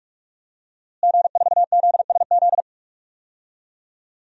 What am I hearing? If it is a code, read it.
Morse code, heard as a keyed tone, about 46 wpm, 700 Hz. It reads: G4ZSZ
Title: DXCC-100 Call Signs - 5 Characters - Repeated 3x - 45wpm